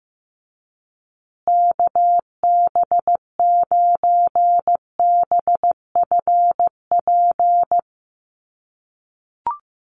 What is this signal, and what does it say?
Morse code, heard as a keyed tone, about 15 wpm, 700 Hz. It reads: KB9BFP
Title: USA Call Signs Repeated 3x 15wpm